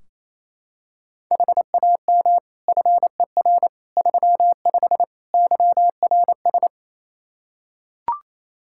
Morse code, heard as a keyed tone, about 28 wpm, 700 Hz. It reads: HAM FER 35 YRS
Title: Practice QSO - Exchange 3 - More Info - 28wpm